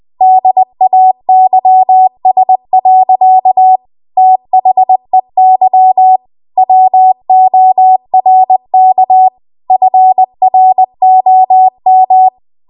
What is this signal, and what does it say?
Morse code, heard as a keyed tone, about 20 wpm, 750 Hz. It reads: DAYS. THEY WORK FROM
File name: Archive / 20 WPM / 20WPM